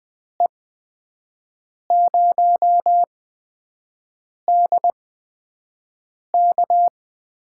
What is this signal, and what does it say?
Morse code, heard as a keyed tone, about 20 wpm, spaced as for 5 wpm, 700 Hz. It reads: E0DK
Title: USA Call Signs - 6 Characters - 8x Character Spacing - 20wpm